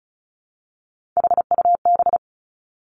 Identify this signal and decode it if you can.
Morse code, heard as a keyed tone, about 36 wpm, 700 Hz. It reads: HUB
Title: Three Challenging Letter-Number Combinations 35wpm